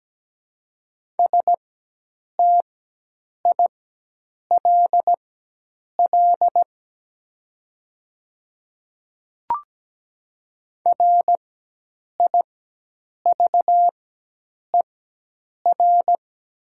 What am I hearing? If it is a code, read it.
Morse code, heard as a keyed tone, about 17 wpm, spaced as for 8 wpm, 700 Hz. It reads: STILL RIVER
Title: 4x Character Spacing - Top 500 Words - 17wpm